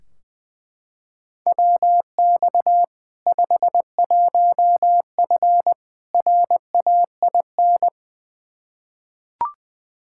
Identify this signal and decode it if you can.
Morse code, heard as a keyed tone, about 21 wpm, 700 Hz. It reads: WX 51F RAIN
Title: K1USN SST QSO Practice 20wpm